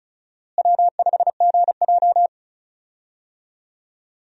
Morse code, heard as a keyed tone, about 34 wpm, 700 Hz. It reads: W5GJ
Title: USA Call Signs - 4 Characters - Repeated 3x - 35wpm